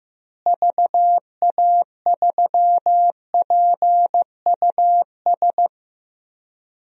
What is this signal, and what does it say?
Morse code, heard as a keyed tone, about 15 wpm, 700 Hz. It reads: VA3PUS
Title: DXCC-100 Call Signs - 6 Characters - Repeated 2x - 15wpm